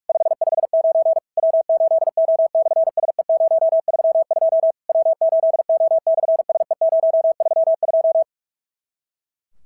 Morse code, heard as a keyed tone, about 46 wpm, 650 Hz. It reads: 559 W8O/SE032 W8O/SE032
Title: SOTA QSO Practice 45wpm